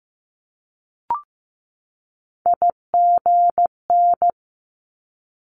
Morse code, heard as a keyed tone, about 15 wpm, 700 Hz. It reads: IGN
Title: Multi-letter Phonemes 15wpm